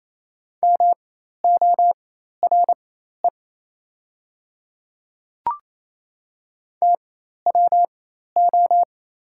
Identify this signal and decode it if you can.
Morse code, heard as a keyed tone, about 28 wpm, spaced as for 13 wpm, 700 Hz. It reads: MORE TWO
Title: 4x Character Spacing - Top 100 Words - 28wpm